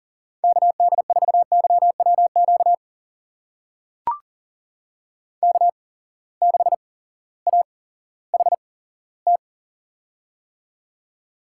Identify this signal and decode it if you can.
Morse code, heard as a keyed tone, about 40 wpm, spaced as for 18 wpm, 700 Hz. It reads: KD4YWQ K6AHT
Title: USA Call Signs - 8x Character Spacing - 40wpm